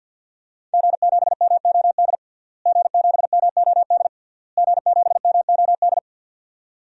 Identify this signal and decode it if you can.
Morse code, heard as a keyed tone, about 50 wpm, 700 Hz. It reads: G7MOD G7MOD G7MOD
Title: DXCC-100 Call Signs - 5 Characters - Repeated 3x - 50wpm